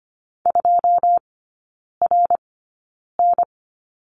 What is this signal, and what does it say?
Morse code, heard as a keyed tone, about 25 wpm, spaced as for 9 wpm, 690 Hz. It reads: 2RN
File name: Lesson 15e - Number 5 - Three Characters 25wpm